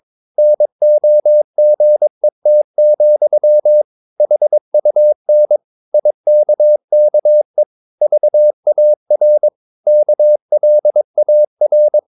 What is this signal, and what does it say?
Morse code, heard as a keyed tone, about 22 wpm, 600 Hz. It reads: NOGET, HUN IKKE VAR KLAR
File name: Ferie QRS 22wpm